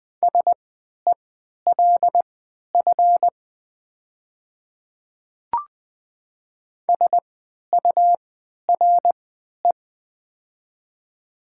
Morse code, heard as a keyed tone, about 20 wpm, spaced as for 11 wpm, 700 Hz. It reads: SELF SURE
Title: 3x Character Spacing - Top 500 Words - 20wpm